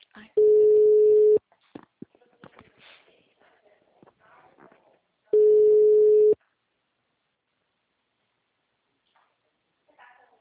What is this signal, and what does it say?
Ringback tone (Europe, 425 Hz), with speech.
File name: out- -2014